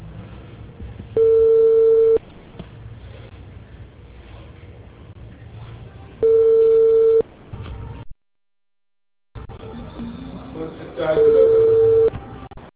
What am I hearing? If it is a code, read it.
Ringback tone (China, 450 Hz), with speech.